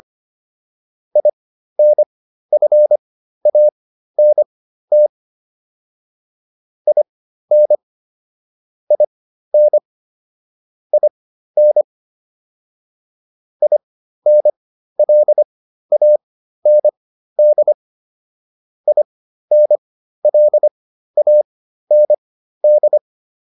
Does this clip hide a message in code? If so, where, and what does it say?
Yes: Morse code from 1.2 to 23.0 s, heard as a keyed tone, about 25 wpm, spaced as for 13 wpm, 600 Hz. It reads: INFANT IN IN IN INLAND INLAND